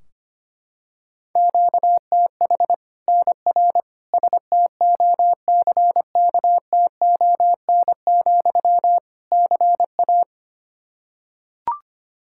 Morse code, heard as a keyed tone, about 25 wpm, 700 Hz. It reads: QTH NR STOCKTON, CA
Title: Practice QSO - Exchange 1 - Beginning - 25wpm